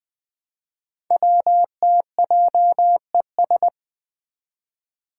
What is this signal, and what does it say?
Morse code, heard as a keyed tone, about 20 wpm, 700 Hz. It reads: WTJES
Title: Code Groups - 5 Letters - 20wpm